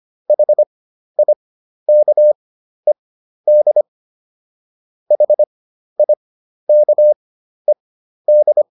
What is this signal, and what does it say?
Morse code, heard as a keyed tone, about 25 wpm, spaced as for 12 wpm, 600 Hz. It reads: HIKED HIKED